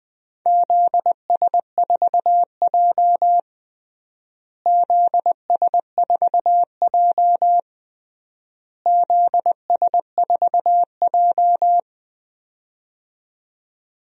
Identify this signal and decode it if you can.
Morse code, heard as a keyed tone, about 21 wpm, 700 Hz. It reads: ZS4J ZS4J ZS4J
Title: DXCC-100 Call Signs - 4 Characters - Repeated 3x - 20wpm